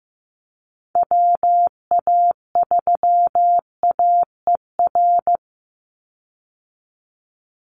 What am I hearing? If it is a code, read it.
Morse code, heard as a keyed tone, about 15 wpm, 700 Hz. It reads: WA3AER
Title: Call Signs Repeated 2x - 15wpm